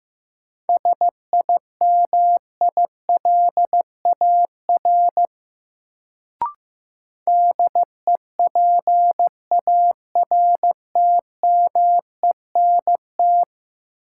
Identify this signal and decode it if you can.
Morse code, heard as a keyed tone, about 15 wpm, 700 Hz. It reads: SIMILAR DEPARTMENT
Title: Sentences from Top 1000 Words - Review Part 5 - New Word Drill - 15wpm